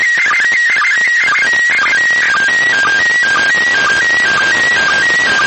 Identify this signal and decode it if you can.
SSTV (slow-scan television), PD 120.
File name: blog / mypics / iss sstv